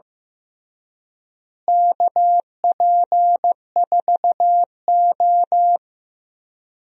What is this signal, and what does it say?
Morse code, heard as a keyed tone, about 15 wpm, 700 Hz. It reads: KP4O
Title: DXCC-100 Call Signs - 4 Characters - Repeated 3x - 15wpm